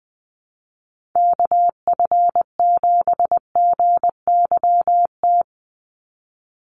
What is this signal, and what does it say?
Morse code, heard as a keyed tone, about 20 wpm, 700 Hz. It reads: KF7GYT
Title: USA Call Signs 20wpm